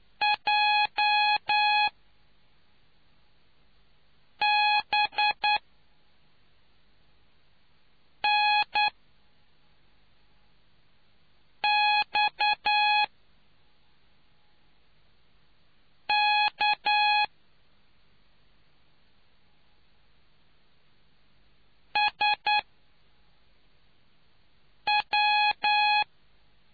Morse code, heard as a keyed tone, about 9 wpm, spaced as for 3 wpm, 840 Hz. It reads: J B N X K SW